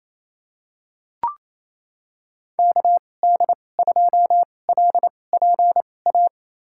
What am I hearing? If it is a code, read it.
Morse code, heard as a keyed tone, about 28 wpm, spaced as for 20 wpm, 700 Hz. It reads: KD2LPA
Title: USA Call Signs - 6 Characters - 2x Character Spacing - 28wpm